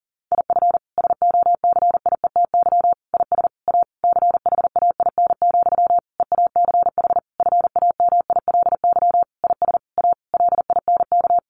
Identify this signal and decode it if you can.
Morse code, heard as a keyed tone, about 40 wpm, 700 Hz. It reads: IF SOCIETY IS A CHAIN, EACH FAMILY IS A LINK
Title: Sentences from Top 1500 Words 40wpm